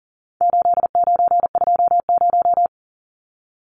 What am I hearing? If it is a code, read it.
Morse code, heard as a keyed tone, about 40 wpm, 700 Hz. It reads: 8920
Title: Numbers - 4-Digit 40wpm